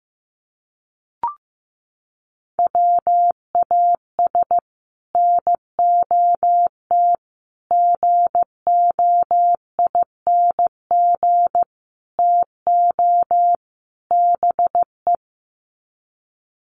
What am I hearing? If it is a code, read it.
Morse code, heard as a keyed tone, about 15 wpm, 700 Hz. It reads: WAS NOT GOING TO BE
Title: Sets of 5 Words Encore 15wpm